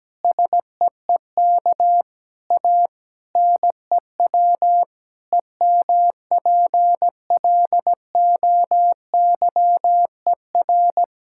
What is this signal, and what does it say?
Morse code, heard as a keyed tone, about 17 wpm, 700 Hz. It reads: SEEK A NEW EMPLOYER
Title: Sentences from Top 1600 Words 17wpm